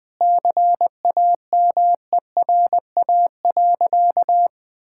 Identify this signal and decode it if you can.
Morse code, heard as a keyed tone, about 21 wpm, 700 Hz. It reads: CAMERA.
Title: Sentences with 3-Word Phrases - Spoken Beginning - 20wpm